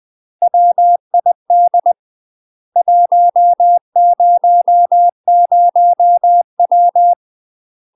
Morse code, heard as a keyed tone, about 20 wpm, 700 Hz. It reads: WID 100W
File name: UK MASTER QSOs TRAINING/20WPM